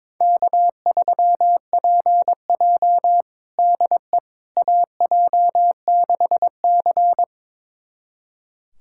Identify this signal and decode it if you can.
Morse code, heard as a keyed tone, about 22 wpm, 700 Hz. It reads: K3PJ DE AJ6C
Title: USA Callsign DE Callsign - 4 Characters - 22wpm